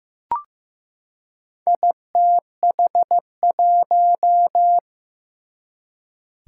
Morse code, heard as a keyed tone, about 15 wpm, 700 Hz. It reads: ITH1
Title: Four Challenging Letter-Number Combinations 15wpm